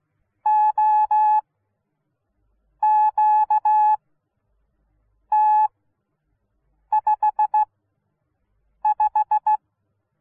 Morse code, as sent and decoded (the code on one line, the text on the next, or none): --- --.- - ..... .....
OQT55